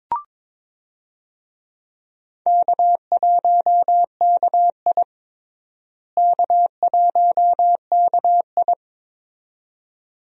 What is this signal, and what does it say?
Morse code, heard as a keyed tone, about 22 wpm, 700 Hz. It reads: K1KI K1KI
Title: USA Call Signs - 4 Characters - Repeated 2x - 22wpm